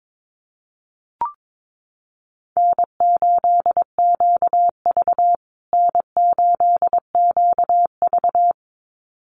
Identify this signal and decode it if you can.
Morse code, heard as a keyed tone, about 22 wpm, 700 Hz. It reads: N8QV N8QV
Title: USA Call Signs Repeated 2x 22wpm